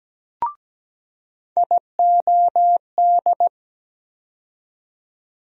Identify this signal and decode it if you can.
Morse code, heard as a keyed tone, about 17 wpm, 700 Hz. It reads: IOD
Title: USA Call Sign Postfixes - 3 Letters 17wpm